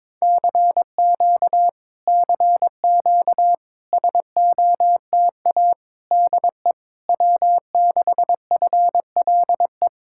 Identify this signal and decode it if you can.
Morse code, heard as a keyed tone, about 22 wpm, 700 Hz. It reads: CQ CQ SOTA DE W6FLE
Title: Speed-Racing Common QSO Phrases - 22wpm